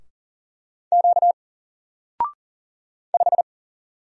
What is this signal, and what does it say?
Morse code, heard as a keyed tone, about 40 wpm, 700 Hz. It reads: Q 5
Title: Single Letter-Number 40wpm